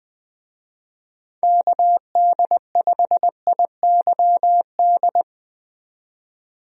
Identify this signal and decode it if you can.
Morse code, heard as a keyed tone, about 20 wpm, 700 Hz. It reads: KD5IYD
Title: USA Call Signs - 6 Characters - 20wpm